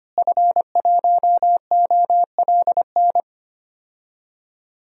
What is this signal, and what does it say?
Morse code, heard as a keyed tone, about 25 wpm, 700 Hz. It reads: F1OLN
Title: DXCC-100 Call Signs - 5 Characters - Repeated 3x - 25wpm